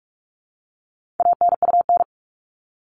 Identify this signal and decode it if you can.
Morse code, heard as a keyed tone, about 44 wpm, 700 Hz. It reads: ANUN
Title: All Cut Numbers - 4-Digit 45wpm